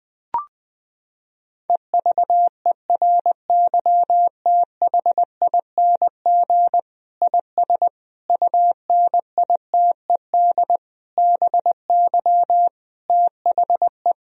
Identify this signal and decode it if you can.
Morse code, heard as a keyed tone, about 20 wpm, 700 Hz. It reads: EVERYTHING IS UNITED BY THE
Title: Sentences from Top 1000 Words - Review Part 4 - 20wpm